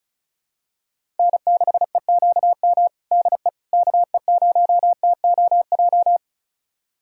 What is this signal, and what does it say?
Morse code, heard as a keyed tone, about 36 wpm, 700 Hz. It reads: N6EQM DE KE0TOJ
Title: USA Callsign DE Callsign 35wpm